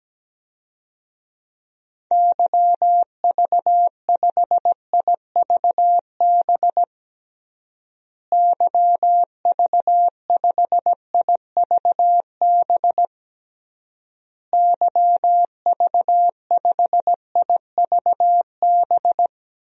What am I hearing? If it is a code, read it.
Morse code, heard as a keyed tone, about 17 wpm, 700 Hz. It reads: YV5IVB YV5IVB YV5IVB
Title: DXCC-100 Call Signs - 6 Characters - Repeated 3x - 17wpm